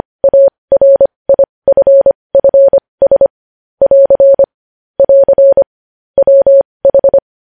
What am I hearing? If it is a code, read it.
Morse code, heard as a keyed tone, about 25 wpm, spaced as for 20 wpm, 550 Hz. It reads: ARIFFS <AR> <AR> WH
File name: podcast / News headlines-1x-25x20